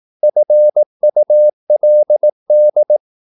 Morse code, heard as a keyed tone, about 18 wpm, 600 Hz. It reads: FULD
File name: Korte ord DK 18wpm